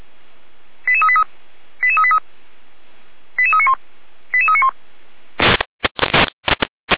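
Five-tone selective call (ZVEI), heard as quick runs of five tones, tones of about 70 ms. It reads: 88383 88383 88391 88391
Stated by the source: ZVEI five-tone call 88335 received by SDR, archived by Feuerwehr Orlamünde (fire brigade).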